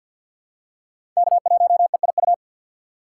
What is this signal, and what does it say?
Morse code, heard as a keyed tone, about 53 wpm, 700 Hz. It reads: K1EIU
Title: USA Call Signs - 5 Characters - 6x Character Spacing - 50wpm